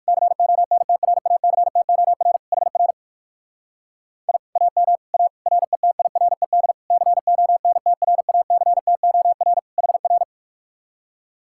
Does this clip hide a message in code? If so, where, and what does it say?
Yes: Morse code from 0.1 to 10.2 s, heard as a keyed tone, about 57 wpm, 700 Hz. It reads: CONTRACTOR HR I AM A RETIRED CONTRACTOR HR